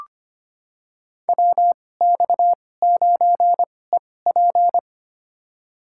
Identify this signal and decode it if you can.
Morse code, heard as a keyed tone, about 25 wpm, spaced as for 18 wpm, 700 Hz. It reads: WX9EP
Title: USA Call Signs - 5 Characters - 2x Character Spacing - 25wpm